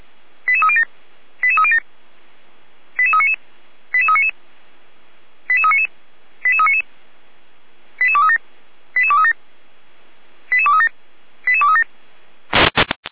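Five-tone selective call (ZVEI), heard as quick runs of five tones, tones of about 70 ms. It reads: 88307 88307 88399 88399 88399 88399 88237 88237 88237 88237
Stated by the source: Five-tone ZVEI selective call received on an SDR.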